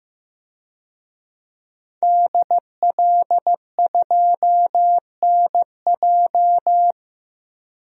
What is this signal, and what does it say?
Morse code, heard as a keyed tone, about 16 wpm, 700 Hz. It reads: DL2NJ
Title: DXCC-100 Call Signs - 5 Characters - Repeated 2x - 15wpm